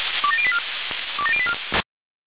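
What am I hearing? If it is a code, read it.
Five-tone selective call (ZVEI), heard as quick runs of five tones, tones of about 70 ms. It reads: 27084 27084
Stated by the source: Five-tone ZVEI selective call received by SDR.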